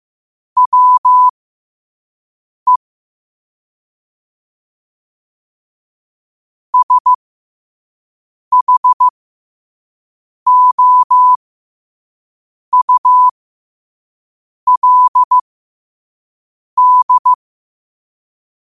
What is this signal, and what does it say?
Morse code, heard as a keyed tone, about 15 wpm, spaced as for 5 wpm, 1000 Hz. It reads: WE SHOULD
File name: qotdfiles / 5wpm